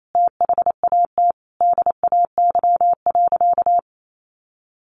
Morse code, heard as a keyed tone, about 28 wpm, 700 Hz. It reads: THAT DAY.
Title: Sentences from Top 1300 Words 28wpm